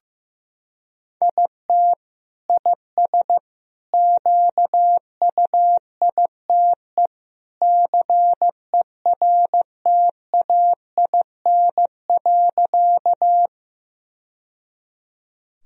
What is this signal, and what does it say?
Morse code, heard as a keyed tone, about 15 wpm, 700 Hz. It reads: IT IS QUITE CERTAIN.
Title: Sentences from Top 1000 Words - Review Part 2 - 15wpm